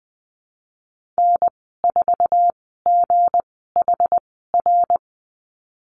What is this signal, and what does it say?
Morse code, heard as a keyed tone, about 19 wpm, spaced as for 14 wpm, 700 Hz. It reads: N4GHR
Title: 207 Lesson 33g - Letter j - Call Signs 20wpm